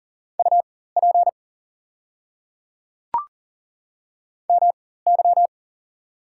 Morse code, heard as a keyed tone, about 39 wpm, spaced as for 19 wpm, 700 Hz. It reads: UP MY
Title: 4x Character Spacing - Top 300 Words - 40wpm